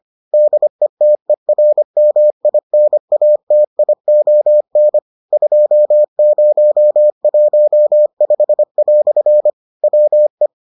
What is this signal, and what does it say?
Morse code, heard as a keyed tone, about 25 wpm, 600 Hz. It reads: DETERMINATION 2015" WE